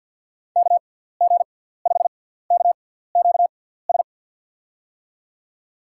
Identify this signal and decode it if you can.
Morse code, heard as a keyed tone, about 53 wpm, 700 Hz. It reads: K G 5 K Q S
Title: USA Call Signs - 6 Characters - 6x Character Spacing - 50wpm